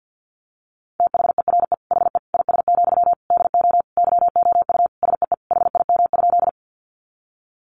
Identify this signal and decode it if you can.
Morse code, heard as a keyed tone, about 50 wpm, 700 Hz. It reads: THERE HE IS, DO YOU SEE HIM?
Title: Sentences from Top 100 Words 50wpm